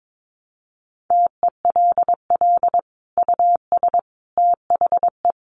Morse code, heard as a keyed tone, about 22 wpm, 700 Hz. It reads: TELL US THE
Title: Sentences from Top 1200 Words 22wpm